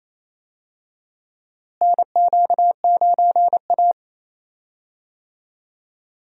Morse code, heard as a keyed tone, about 28 wpm, 700 Hz. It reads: NQ9A
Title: USA Call Signs - 4 Characters - Repeated 2x - 28wpm